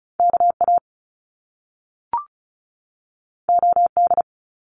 Morse code, heard as a keyed tone, about 36 wpm, 700 Hz. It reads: KA OD
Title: Top Two Letter Combinations in Words 35wpm